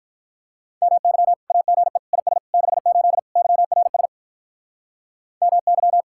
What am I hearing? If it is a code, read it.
Morse code, heard as a keyed tone, about 47 wpm, 700 Hz. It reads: MY AGE IS 68 YRS MY